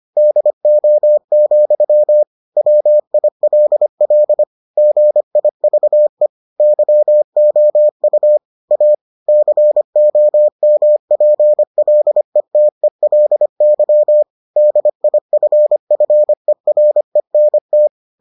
Morse code, heard as a keyed tone, about 25 wpm, 600 Hz. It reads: DO, WILL GIVE YOU A COMPLETELY DIFFERENT